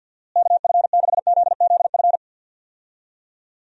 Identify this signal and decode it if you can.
Morse code, heard as a keyed tone, about 51 wpm, 700 Hz. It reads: KF6ZZF
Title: USA Call Signs - 6 Characters - Repeated 3x - 50wpm